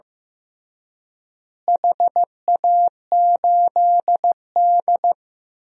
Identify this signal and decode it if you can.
Morse code, heard as a keyed tone, about 16 wpm, 700 Hz. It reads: HA8D
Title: DXCC-100 Call Signs - 4 Characters - Repeated 3x - 15wpm